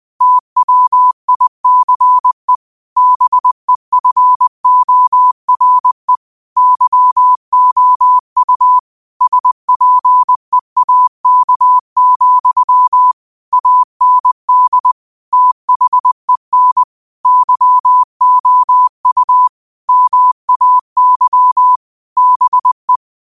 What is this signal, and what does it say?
Morse code, heard as a keyed tone, about 20 wpm, 1000 Hz. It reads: TWICE BEFORE YOU SPEAK, AND THEN YOU MAY BE
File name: qotdfiles / 20wpm